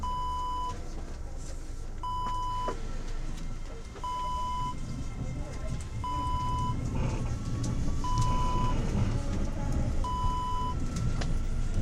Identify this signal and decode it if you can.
Speech. There is no code.